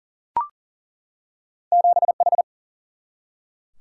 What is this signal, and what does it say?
Morse code, heard as a keyed tone, about 40 wpm, 700 Hz. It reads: 7H
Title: Single-Letter Single-Number 40wpm